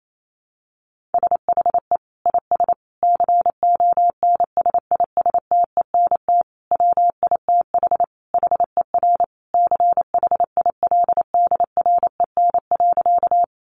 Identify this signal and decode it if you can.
Morse code, heard as a keyed tone, about 28 wpm, 700 Hz. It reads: SHE IS CONSISTENT WITH HER CHILDREN.
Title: Sentences from Top 1500 Words 28wpm